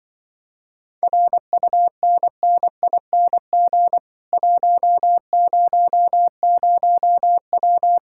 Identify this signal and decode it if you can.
Morse code, heard as a keyed tone, about 24 wpm, 700 Hz. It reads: RUNNING 100W
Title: Speed-Racing Common QSO Phrases - 20wpm